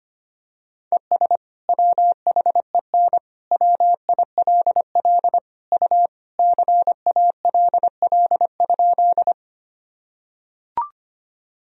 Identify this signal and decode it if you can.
Morse code, heard as a keyed tone, about 25 wpm, 700 Hz. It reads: ES WHEN WILL U CALL?
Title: Sentences from Top 100 Words and Abbreviations 25wpm